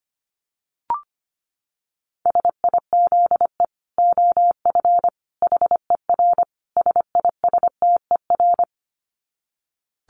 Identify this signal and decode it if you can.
Morse code, heard as a keyed tone, about 25 wpm, 700 Hz. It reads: SIZE OF HER SISTER
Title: Dit Challenge 25wpm